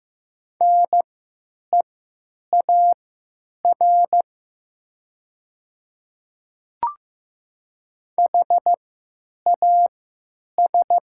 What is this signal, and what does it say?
Morse code, heard as a keyed tone, about 15 wpm, spaced as for 8 wpm, 700 Hz. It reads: NEAR HAS
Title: 3x Character Spacing - Top 200 Words - 15wpm